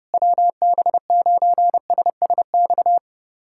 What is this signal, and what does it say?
Morse code, heard as a keyed tone, about 30 wpm, 700 Hz. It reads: WB9SSX
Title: USA Callsign DE Callsign - 6 Characters - 30wpm